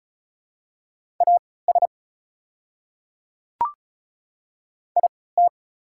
Morse code, heard as a keyed tone, about 36 wpm, spaced as for 20 wpm, 700 Hz. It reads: AS IT
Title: 3x Character Spacing - Top 200 Words - 35wpm